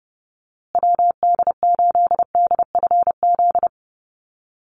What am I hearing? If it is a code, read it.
Morse code, heard as a keyed tone, about 30 wpm, 700 Hz. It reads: WD8DFZ
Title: USA Call Signs Repeated 2x 30wpm